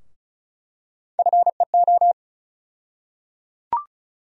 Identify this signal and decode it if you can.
Morse code, heard as a keyed tone, about 36 wpm, 700 Hz. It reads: FEO
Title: USA Call Sign Postfixes - 3 Letters 35wpm